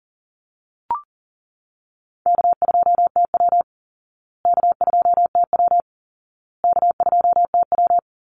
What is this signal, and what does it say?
Morse code, heard as a keyed tone, about 40 wpm, 700 Hz. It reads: K2TW K2TW K2TW
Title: Easy Call Signs Repeated 3x - 40wpm